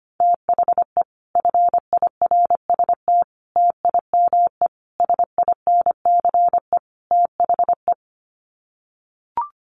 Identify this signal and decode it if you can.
Morse code, heard as a keyed tone, about 25 wpm, 700 Hz. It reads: THE FIRST TIME SINCE THE
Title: Sets of 5 Words Encore 25wpm